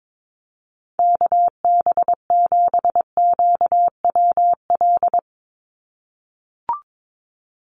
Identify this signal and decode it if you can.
Morse code, heard as a keyed tone, about 22 wpm, 700 Hz. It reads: KB7QWL
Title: USA Call Signs 22wpm